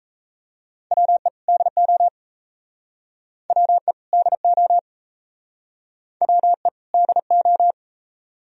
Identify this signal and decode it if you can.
Morse code, heard as a keyed tone, about 36 wpm, 700 Hz. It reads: WE DO WE DO WE DO